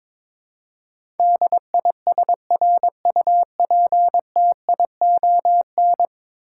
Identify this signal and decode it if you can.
Morse code, heard as a keyed tone, about 22 wpm, 700 Hz. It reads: DISRUPTION
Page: morse-code-ninja-practice-22wpm